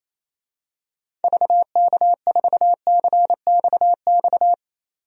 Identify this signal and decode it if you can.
Morse code, heard as a keyed tone, about 28 wpm, 700 Hz. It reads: VK4CXX
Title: DXCC-100 Call Signs - 6 Characters - 28wpm